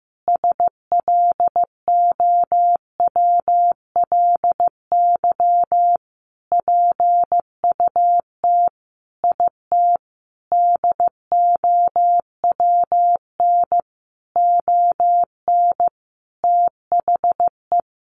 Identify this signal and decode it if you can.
Morse code, heard as a keyed tone, about 15 wpm, 700 Hz. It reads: SLOWLY PUT IT DOWN ON THE
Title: Sentences from Top 1100 Words 15wpm